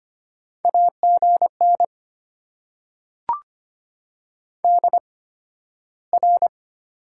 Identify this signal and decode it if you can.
Morse code, heard as a keyed tone, about 25 wpm, spaced as for 13 wpm, 700 Hz. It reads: AGN DR
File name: Top 90 QSO Elements - 8x Character Spacing - 25wpm